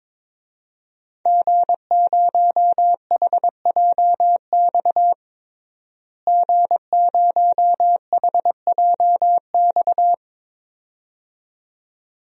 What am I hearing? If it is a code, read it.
Morse code, heard as a keyed tone, about 21 wpm, 700 Hz. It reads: G0HJX G0HJX
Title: DXCC-100 Call Signs - 5 Characters - Repeated 2x - 22wpm